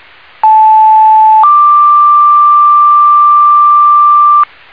Two-tone page, heard as a one-second tone, then a three-second tone, about 870 then 1180 Hz, with noise. It is heard from 0.4 to 4.4 s.